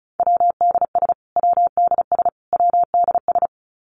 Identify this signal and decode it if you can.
Morse code, heard as a keyed tone, about 34 wpm, 700 Hz. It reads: WDS WDS WDS
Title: Abbreviations Repeated 3x - 35wpm